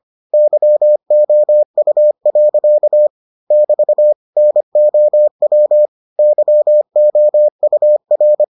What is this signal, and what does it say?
Morse code, heard as a keyed tone, about 25 wpm, 600 Hz. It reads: YOU. <BT> NOW YOUR